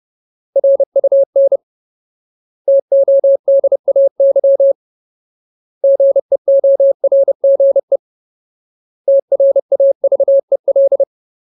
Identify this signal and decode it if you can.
Morse code, heard as a keyed tone, about 30 wpm, 550 Hz. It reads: RUN TODAY GEORGE TRAVEL
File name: Ultimate Word List / 30WPM Ultimate